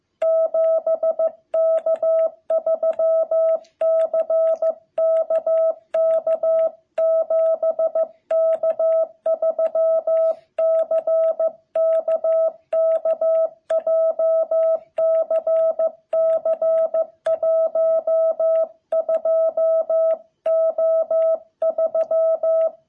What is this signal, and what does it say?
Morse code, heard as a keyed tone, about 15 wpm, 660 Hz. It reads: 7K3CKK 7K3CKK JCC12O3
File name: jh8cbh / 2021cw